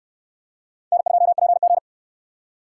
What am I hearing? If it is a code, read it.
Morse code, heard as a keyed tone, about 69 wpm, 700 Hz. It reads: N2PG